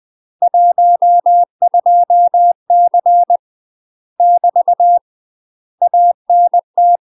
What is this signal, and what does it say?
Morse code, heard as a keyed tone, about 20 wpm, 700 Hz. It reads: 12C <BT> ANT
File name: UK MASTER QSOs TRAINING / 20WPM / UK MASTER QSOs 0053